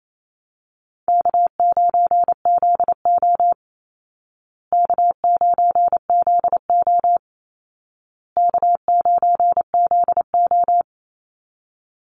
Morse code, heard as a keyed tone, about 28 wpm, 700 Hz. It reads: K9ZO K9ZO K9ZO
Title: Easy Call Signs Repeated 3x - 28wpm